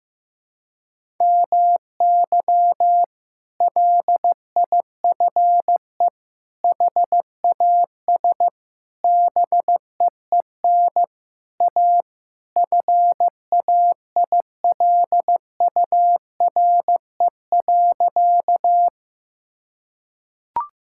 Morse code, heard as a keyed tone, about 16 wpm, 700 Hz. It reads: MY LIFE HAS BEEN A FAILURE.
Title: Sentences from Top 1000 Words - Review Part 6 - 15wpm